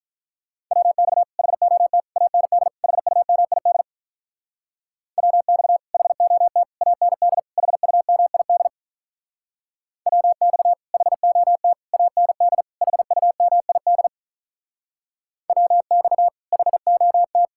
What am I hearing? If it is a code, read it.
Morse code, heard as a keyed tone, about 45 wpm, 700 Hz. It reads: WX HOT AND HUMID WX HOT AND HUMID WX HOT AND HUMID WX HOT